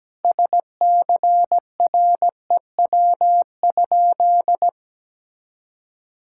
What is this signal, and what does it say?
Morse code, heard as a keyed tone, about 17 wpm, 700 Hz. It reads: SCREW?
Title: Sentences from Top 1600 Words 17wpm